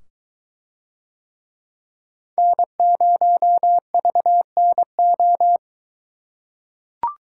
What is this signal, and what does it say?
Morse code, heard as a keyed tone, about 22 wpm, 700 Hz. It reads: N0VNO